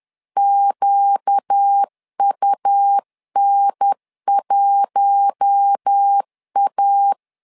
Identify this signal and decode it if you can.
Morse code, heard as a keyed tone, about 10 wpm, 800 Hz. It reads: QUN1A
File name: LEZIONI / 5 LBVC / 5 L 10wpm 2259